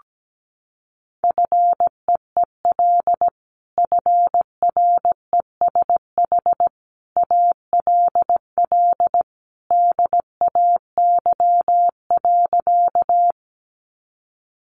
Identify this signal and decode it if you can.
Morse code, heard as a keyed tone, about 17 wpm, 700 Hz. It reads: FEEL FRESH ALL DAY.
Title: Sentences from Top 1200 Words 17wpm